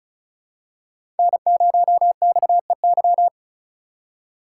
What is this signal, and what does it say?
Morse code, heard as a keyed tone, about 36 wpm, 700 Hz. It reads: N0XEY
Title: USA Call Signs - 4x Character Spacing - 35wpm